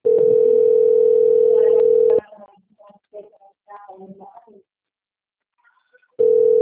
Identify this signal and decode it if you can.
Ringback tone (North America), 440 + 480 Hz.